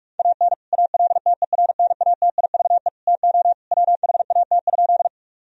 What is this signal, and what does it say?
Morse code, heard as a keyed tone, about 44 wpm, 700 Hz. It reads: AN ALTERNATIVE TO WHAT?
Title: Sentences from Top 1000 Words - Review Part 4 - 45wpm